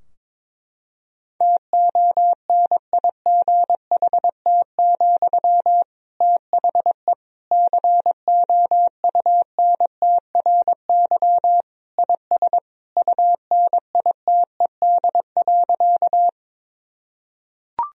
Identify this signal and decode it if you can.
Morse code, heard as a keyed tone, about 21 wpm, 700 Hz. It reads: TONIGHT, THE COUNTRY IS UNITED.
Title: Sentences from Top 1000 Words - Review Part 4 - 22wpm